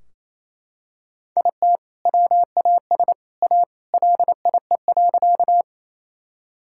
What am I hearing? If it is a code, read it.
Morse code, heard as a keyed tone, about 28 wpm, 700 Hz. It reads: IT WAS A LIE.
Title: Sentences from Top 1600 Words 28wpm